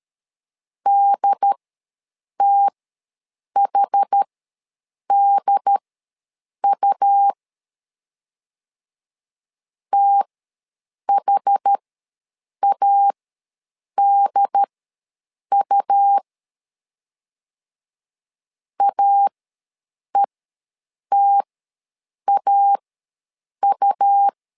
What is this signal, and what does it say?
Morse code, heard as a keyed tone, about 13 wpm, spaced as for 7 wpm, 800 Hz. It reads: DTHDU THADU AETAU
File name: LEZIONI / 2 UHDN / 2 UHD ws 12wpm 1722